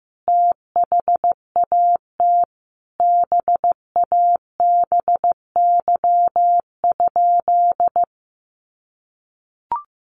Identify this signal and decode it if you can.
Morse code, heard as a keyed tone, about 15 wpm, 700 Hz. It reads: THAT BABY?
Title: Sentences from Top 1300 Words 15wpm